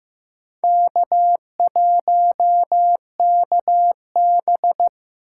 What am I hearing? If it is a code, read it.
Morse code, heard as a keyed tone, about 15 wpm, 700 Hz. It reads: K1KB
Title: USA Call Signs - 4 Characters - 4x Character Spacing - 15wpm